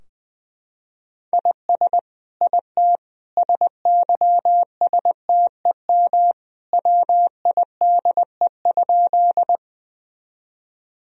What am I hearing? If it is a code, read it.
Morse code, heard as a keyed tone, about 20 wpm, 700 Hz. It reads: IS IT SYSTEM WIDE?
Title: Sentences from Top 1000 Words - Review Part 3 - 20wpm